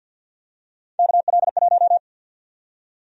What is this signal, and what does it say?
Morse code, heard as a keyed tone, about 48 wpm, 700 Hz. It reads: KL1
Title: USA Call Sign Prefixes - Repeated 2x - 50wpm